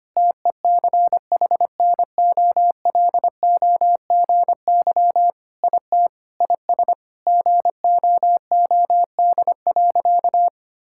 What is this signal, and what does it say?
Morse code, heard as a keyed tone, about 25 wpm, 700 Hz. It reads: TECHNOLOGY IT IS GOOD.
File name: practice / 25wpm / Sentences with 3-Word Phrases - 25wpm